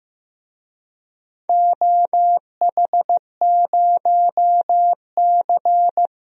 Morse code, heard as a keyed tone, about 15 wpm, 700 Hz. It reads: OH0C
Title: DXCC-100 Call Signs - 4 Characters - 15wpm